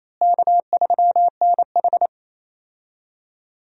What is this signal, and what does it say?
Morse code, heard as a keyed tone, about 28 wpm, 700 Hz. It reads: K3NH
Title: USA Call Signs - 4 Characters - Repeated 3x - 28wpm